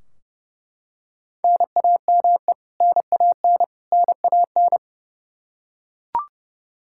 Morse code, heard as a keyed tone, about 30 wpm, 700 Hz. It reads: NAME NAN NAN
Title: K1USN SST QSO Practice 30wpm